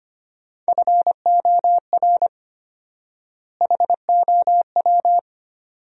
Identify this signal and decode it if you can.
Morse code, heard as a keyed tone, about 25 wpm, 700 Hz. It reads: FOR HOW
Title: 4x Word-Spacing - Sentences from Top 200 Words - 25wpm